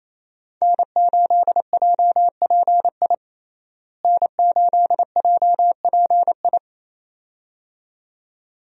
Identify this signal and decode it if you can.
Morse code, heard as a keyed tone, about 28 wpm, 700 Hz. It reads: N8JPI N8JPI
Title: USA Call Signs - 5 Characters - Repeated 2x - 28wpm